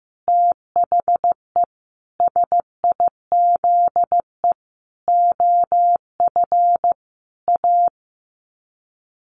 Morse code, heard as a keyed tone, about 15 wpm, 700 Hz. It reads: THE SIZE OF A
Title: Sets of 5 Words 15wpm